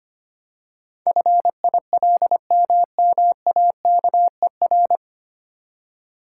Morse code, heard as a keyed tone, about 25 wpm, 700 Hz. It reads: FILMMAKER
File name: Nine Letter Words - 25wpm